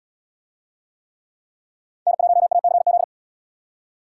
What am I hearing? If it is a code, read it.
Morse code, heard as a keyed tone, about 74 wpm, 700 Hz. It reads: N1ACZ